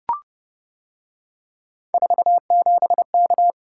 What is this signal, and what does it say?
Morse code, heard as a keyed tone, about 29 wpm, 700 Hz. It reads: 47K